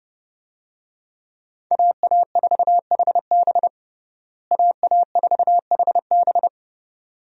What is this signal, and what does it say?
Morse code, heard as a keyed tone, about 32 wpm, 700 Hz. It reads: AA4HB AA4HB
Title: USA Call Signs - 5 Characters - Repeated 2x - 30wpm